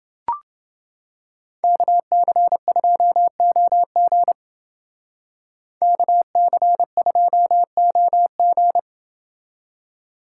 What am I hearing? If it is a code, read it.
Morse code, heard as a keyed tone, about 28 wpm, 700 Hz. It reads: KC2OG KC2OG